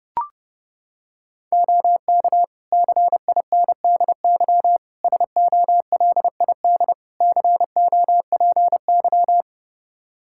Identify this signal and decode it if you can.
Morse code, heard as a keyed tone, about 30 wpm, 700 Hz. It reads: OK CINDY SOLID COPY
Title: K1USN SST QSO Practice 30wpm